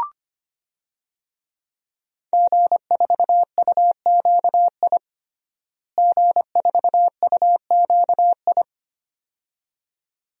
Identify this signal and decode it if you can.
Morse code, heard as a keyed tone, about 25 wpm, 700 Hz. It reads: G4UQI G4UQI
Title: DXCC-100 Call Signs - 5 Characters - Repeated 2x - 25wpm